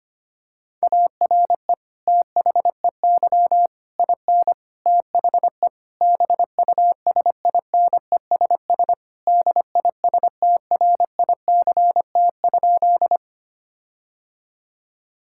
Morse code, heard as a keyed tone, about 25 wpm, 700 Hz. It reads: ARE THEY IN THE BUSINESS DISTRICT?
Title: Sentences from Top 1600 Words 25wpm